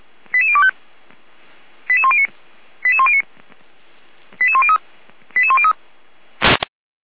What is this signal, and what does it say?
Five-tone selective call (ZVEI), heard as quick runs of five tones, tones of about 70 ms. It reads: 88025 88109 88109 88103 88103